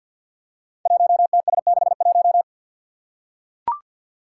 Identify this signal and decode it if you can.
Morse code, heard as a keyed tone, about 45 wpm, 700 Hz. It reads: 1TS61
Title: Five Challenging Letter-Number Combinations 50wpm